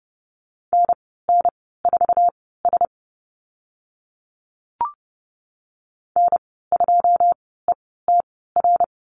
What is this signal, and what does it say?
Morse code, heard as a keyed tone, about 30 wpm, spaced as for 17 wpm, 700 Hz. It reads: NN4S N2ETR
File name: Lesson 14g - Number 2 - Call Signs 30wpm